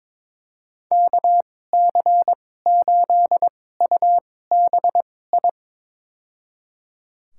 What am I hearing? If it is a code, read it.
Morse code, heard as a keyed tone, about 22 wpm, spaced as for 16 wpm, 700 Hz. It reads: KC8UBI
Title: USA Call Signs - 2x Character Spacing - 22wpm